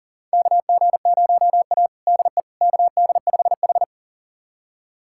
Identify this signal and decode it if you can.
Morse code, heard as a keyed tone, about 40 wpm, 700 Hz. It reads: KG0A DE KD5H
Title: USA Callsign DE Callsign - 4 Characters - 40wpm